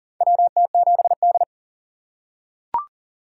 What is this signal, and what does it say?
Morse code, heard as a keyed tone, about 40 wpm, 700 Hz. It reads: WT7D
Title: USA Call Signs - 4 Characters - 2x Character Spacing - 40wpm